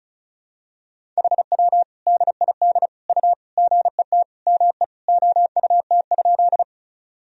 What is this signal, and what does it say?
Morse code, heard as a keyed tone, about 34 wpm, 700 Hz. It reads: HW DID U GET ME OUT?
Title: Sentences from Top 200 Words and Abbreviations 35wpm